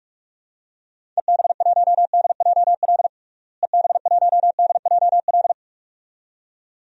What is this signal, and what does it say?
Morse code, heard as a keyed tone, about 46 wpm, 700 Hz. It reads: EB1DJL EB1DJL
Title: DXCC-100 Call Signs - 6 Characters - Repeated 2x - 45wpm